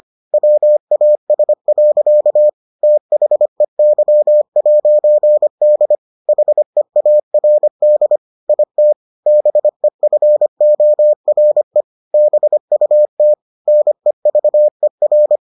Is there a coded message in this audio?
Yes: Morse code from 0.3 to 15.4 s, heard as a keyed tone, about 25 wpm, 600 Hz. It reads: WAS. THEY'D HEARD IT BEFORE BUT NEVER